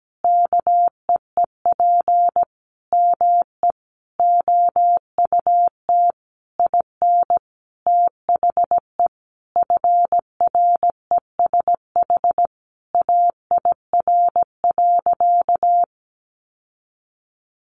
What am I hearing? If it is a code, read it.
Morse code, heard as a keyed tone, about 17 wpm, 700 Hz. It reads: KEEP ME OUT IN THE FRESH AIR.
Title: Sentences from Top 1200 Words 17wpm